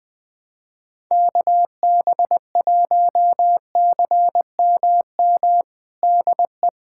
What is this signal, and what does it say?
Morse code, heard as a keyed tone, about 21 wpm, 700 Hz. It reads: KB1CMM DE
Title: USA Callsign DE Callsign - 6 Characters - 20wpm